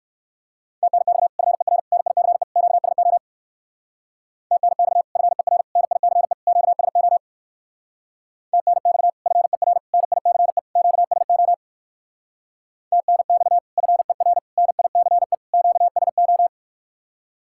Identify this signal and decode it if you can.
Morse code, heard as a keyed tone, about 66 wpm, 700 Hz. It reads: TNX FER NICE QSO TNX FER NICE QSO TNX FER NICE QSO TNX FER NICE QSO